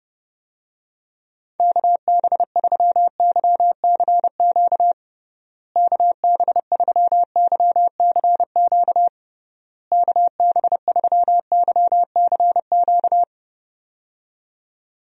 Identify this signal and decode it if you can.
Morse code, heard as a keyed tone, about 29 wpm, 700 Hz. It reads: KB3YCQ KB3YCQ KB3YCQ
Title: USA Call Signs - 6 Characters - Repeated 3x - 30wpm